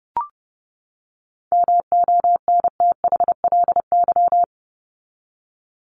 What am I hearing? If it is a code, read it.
Morse code, heard as a keyed tone, about 30 wpm, 700 Hz. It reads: MONTHLY
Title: LY Suffix - Top 100 Words with Base Words 30wpm